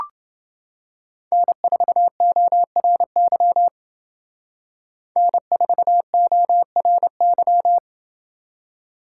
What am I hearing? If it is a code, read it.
Morse code, heard as a keyed tone, about 29 wpm, 700 Hz. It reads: N4ORY N4ORY